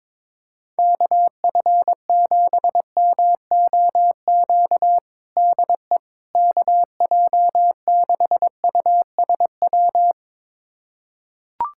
Morse code, heard as a keyed tone, about 22 wpm, 700 Hz. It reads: KF7MOQ DE KJ6USW
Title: USA Callsign DE Callsign - 6 Characters - 22wpm